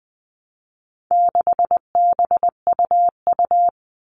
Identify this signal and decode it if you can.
Morse code, heard as a keyed tone, about 20 wpm, 700 Hz. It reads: 6BUU
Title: All Cut Numbers - 4-Digit 20wpm